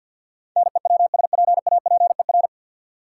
Morse code, heard as a keyed tone, about 52 wpm, 700 Hz. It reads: NEWSPAPER
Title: Sentences from Top 1000 Words - Review Part 4 - New Word Drill - 50wpm